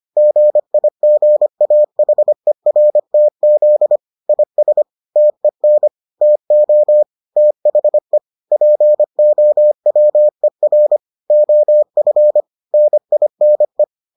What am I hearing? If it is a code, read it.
Morse code, heard as a keyed tone, about 25 wpm, 600 Hz. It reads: GIGAHERTZ IS TEN TO THE POWER OF NINE